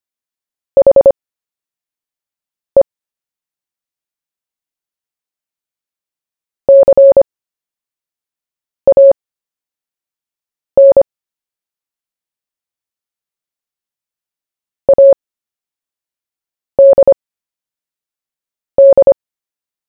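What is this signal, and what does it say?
Morse code, heard as a keyed tone, about 25 wpm, spaced as for 5 wpm, 550 Hz. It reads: HE CAN ADD
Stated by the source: CW story-telling practice, Short Story 1 sent at 5 wpm, by VK4JE, Queensland, Australia.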